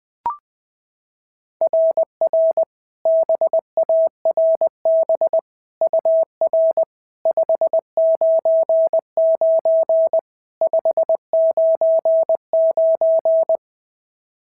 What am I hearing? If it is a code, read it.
Morse code, heard as a keyed tone, about 18 wpm, 650 Hz. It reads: RR BARB UR 599 599
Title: SOTA QSO Practice 20wpm